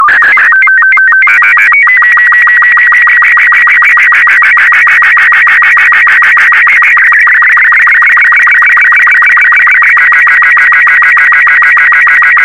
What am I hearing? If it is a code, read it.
SSTV (slow-scan television), Robot 36.
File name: sstv / R36-VK4MSL